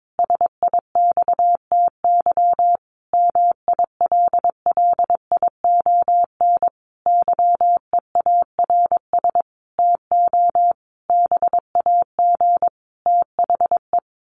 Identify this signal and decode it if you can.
Morse code, heard as a keyed tone, about 22 wpm, 700 Hz. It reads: SIXTY MILLION YEARS TO BAG THE
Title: A Sound Of Thunder 22wpm